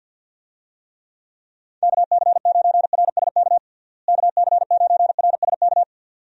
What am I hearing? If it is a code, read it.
Morse code, heard as a keyed tone, about 53 wpm, 700 Hz. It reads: KC9RSK KC9RSK
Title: USA Call Signs - 6 Characters - Repeated 2x - 50wpm